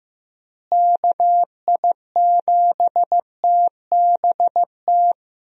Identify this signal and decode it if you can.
Morse code, heard as a keyed tone, about 15 wpm, 700 Hz. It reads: KI7TBT
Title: USA Call Signs - 6 Characters - Repeated 3x - 15wpm